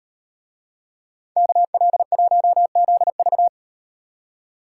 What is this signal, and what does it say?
Morse code, heard as a keyed tone, about 37 wpm, 700 Hz. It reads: KL1ZV